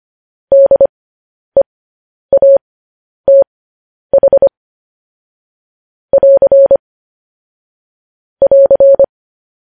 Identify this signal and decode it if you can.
Morse code, heard as a keyed tone, about 25 wpm, spaced as for 10 wpm, 550 Hz. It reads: DEATH <AR> <AR>
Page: News Headlines in Morse Code at 10 WPM